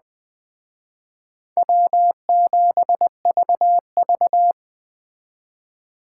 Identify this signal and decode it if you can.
Morse code, heard as a keyed tone, about 20 wpm, 700 Hz. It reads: W7VV
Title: Speed Racing - USA 1x2 Call Signs 17wpm